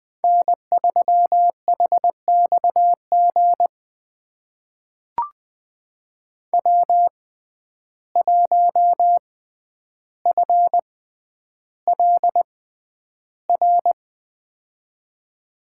Morse code, heard as a keyed tone, about 21 wpm, spaced as for 10 wpm, 700 Hz. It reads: N3HXG W1FLR
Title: USA Call Signs - 5 Characters - 6x Character Spacing - 20wpm